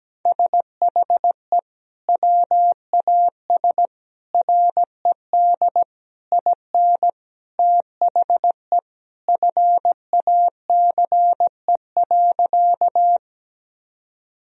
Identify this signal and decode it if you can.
Morse code, heard as a keyed tone, about 17 wpm, 700 Hz. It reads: SHE WAS RED IN THE FACE.
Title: Sentences from Top 1000 Words - Review Part 2 - 17wpm